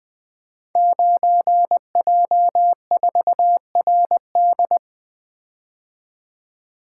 Morse code, heard as a keyed tone, about 20 wpm, 700 Hz. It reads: 9J4RD
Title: Code Groups - 5 Letter:Numbers - 20wpm